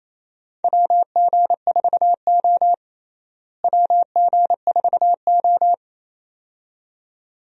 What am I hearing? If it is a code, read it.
Morse code, heard as a keyed tone, about 28 wpm, 700 Hz. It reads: WG4O WG4O
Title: USA Call Signs - 4 Characters - Repeated 2x - 28wpm